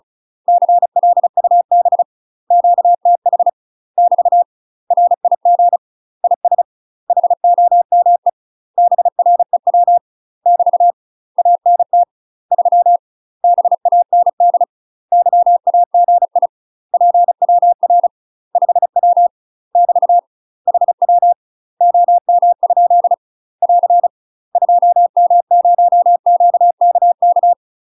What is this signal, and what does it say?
Morse code, heard as a keyed tone, about 36 wpm, 700 Hz. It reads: CLUB QTH <BT> RIG IS HOME BREW <BT> ANT 3 BAND YAGI PWR 5W <BT> HW OM? <AR> 2M0QKK
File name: UK MASTER QSOs TRAINING / 35WPM / UK MASTER QSOs 0010